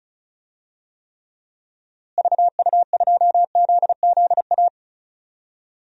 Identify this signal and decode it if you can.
Morse code, heard as a keyed tone, about 34 wpm, 700 Hz. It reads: VU2ZZA
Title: DXCC-100 Call Signs - 6 Characters - Repeated 3x - 35wpm